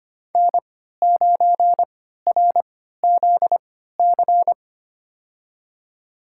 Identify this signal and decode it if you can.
Morse code, heard as a keyed tone, about 25 wpm, spaced as for 14 wpm, 700 Hz. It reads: N9RZC
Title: USA Call Signs - 5 Characters - 3x Character Spacing - 25wpm